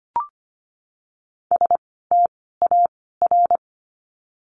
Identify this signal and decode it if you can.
Morse code, heard as a keyed tone, about 25 wpm, 700 Hz. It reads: S T A R
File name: course / 25:10wpm / 094 Lesson 16f - Letter c - Words 25wpm